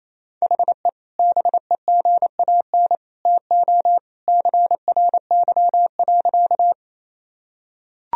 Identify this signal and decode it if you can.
Morse code, heard as a keyed tone, about 28 wpm, 700 Hz. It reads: HE BEGAN TO CRY.
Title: Sentences from Top 1000 Words - Review Part 2 - 28wpm